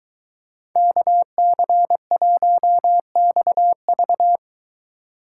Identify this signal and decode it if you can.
Morse code, heard as a keyed tone, about 23 wpm, 700 Hz. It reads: KC1XV